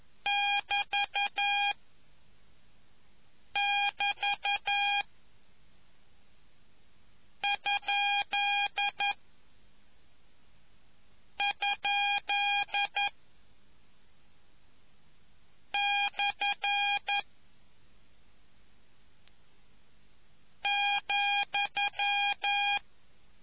Morse code, heard as a keyed tone, about 11 wpm, 2450 Hz. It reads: <BT> <BT> ? ? / ,